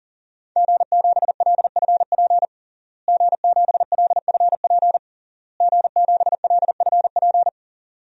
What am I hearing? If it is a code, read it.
Morse code, heard as a keyed tone, about 40 wpm, 700 Hz. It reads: G7LFP G7LFP G7LFP
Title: DXCC-100 Call Signs - 5 Characters - Repeated 3x - 40wpm